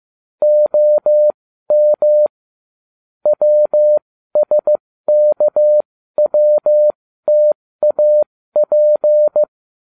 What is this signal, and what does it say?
Morse code, heard as a keyed tone, about 15 wpm, spaced as for 13 wpm, 600 Hz. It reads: OM WSKWTAP